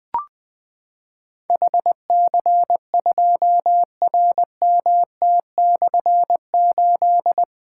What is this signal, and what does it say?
Morse code, heard as a keyed tone, about 20 wpm, 700 Hz. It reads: HC2RMT/8
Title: Difficult Call Signs 20wpm